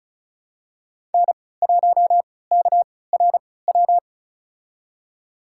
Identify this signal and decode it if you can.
Morse code, heard as a keyed tone, about 36 wpm, spaced as for 20 wpm, 700 Hz. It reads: N1KRW
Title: USA Call Signs - 3x Character Spacing - 35wpm